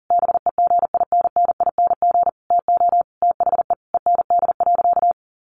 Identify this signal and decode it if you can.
Morse code, heard as a keyed tone, about 40 wpm, 700 Hz. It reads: BEGINNING TO THE END.
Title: Sentences from Top 1300 Words 40wpm